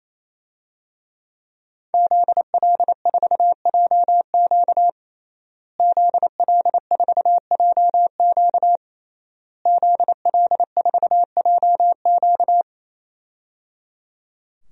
Morse code, heard as a keyed tone, about 28 wpm, 690 Hz. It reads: ZL4JQ ZL4JQ ZL4JQ
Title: DXCC-100 Call Signs - 5 Characters - Repeated 3x - 28wpm